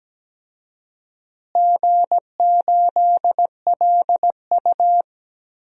Morse code, heard as a keyed tone, about 17 wpm, 700 Hz. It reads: G8LU
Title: DXCC-100 Call Signs - 4 Characters - Repeated 3x - 17wpm